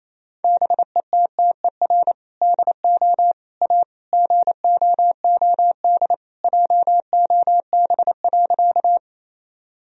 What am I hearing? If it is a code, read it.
Morse code, heard as a keyed tone, about 28 wpm, 700 Hz. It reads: BETTER DO A GOOD JOB.
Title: Sentences from Top 1000 Words - Review Part 3 - 28wpm